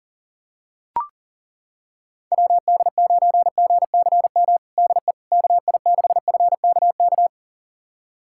Morse code, heard as a keyed tone, about 37 wpm, 700 Hz. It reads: WD9GCM DE KI6FKK
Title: USA Callsign DE Callsign - 6 Characters - 40wpm